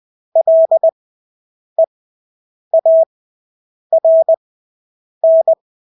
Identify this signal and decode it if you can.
Morse code, heard as a keyed tone, about 20 wpm, spaced as for 8 wpm, 650 Hz. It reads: LEARN
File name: Engelske ord 20/8 wpm